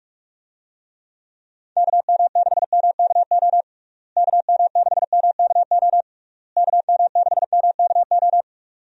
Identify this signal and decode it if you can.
Morse code, heard as a keyed tone, about 46 wpm, 700 Hz. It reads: KM6MKO KM6MKO KM6MKO
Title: USA Call Signs - 6 Characters - Repeated 3x - 45wpm